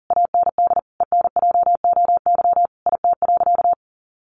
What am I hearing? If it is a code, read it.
Morse code, heard as a keyed tone, about 40 wpm, 700 Hz. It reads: AND ENJOY IT.
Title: Sentences from Top 1400 Words 40wpm